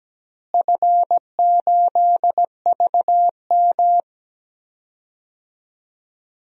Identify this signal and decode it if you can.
Morse code, heard as a keyed tone, about 17 wpm, 700 Hz. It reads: F8VM
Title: DXCC-100 Call Signs - 4 Characters - Repeated 3x - 17wpm